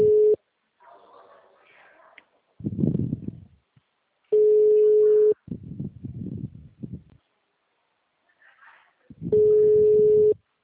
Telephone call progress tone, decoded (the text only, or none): ringback tone